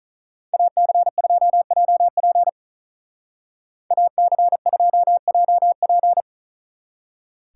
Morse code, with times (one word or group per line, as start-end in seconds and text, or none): 0.5-2.5 AC2JP
3.9-6.2 AC2JP